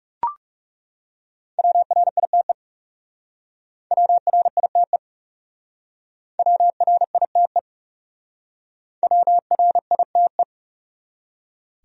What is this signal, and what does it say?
Morse code, heard as a keyed tone, about 39 wpm, 700 Hz. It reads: WRITE WRITE WRITE WRITE